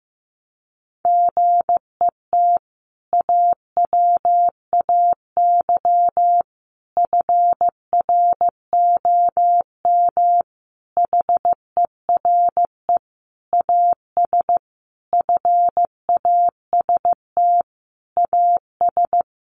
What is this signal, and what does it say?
Morse code, heard as a keyed tone, about 15 wpm, 700 Hz. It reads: GET AWAY FROM HERE AS FAST AS
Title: Sentences from Top 600 Words 15wpm